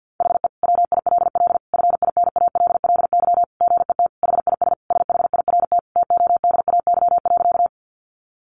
Morse code, heard as a keyed tone, about 50 wpm, 700 Hz. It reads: HE WILL FINALLY GET HIS SHIRT TODAY.
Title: Additional Easy Phrases 50wpm